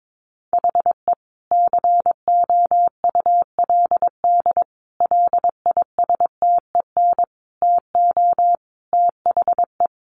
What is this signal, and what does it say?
Morse code, heard as a keyed tone, about 22 wpm, 700 Hz. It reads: HE COULD LISTEN TO THE
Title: Sentences from Top 1500 Words 22wpm